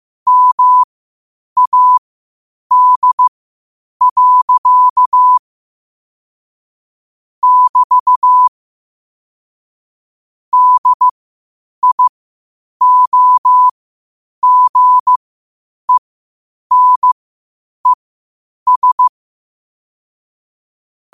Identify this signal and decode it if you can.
Morse code, heard as a keyed tone, about 15 wpm, spaced as for 8 wpm, 1000 Hz. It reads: MAD. <BT> DIOGENES
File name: qotdfiles / 8wpm